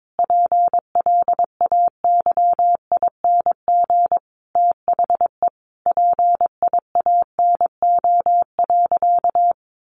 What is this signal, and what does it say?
Morse code, heard as a keyed tone, about 22 wpm, 700 Hz. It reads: PLAYING THE PIANO.
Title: Additional Easy Phrases 22wpm